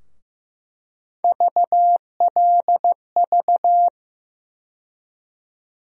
Morse code, heard as a keyed tone, about 15 wpm, 700 Hz. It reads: VLV
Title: USA Call Sign Postfixes - 3 Letters 15wpm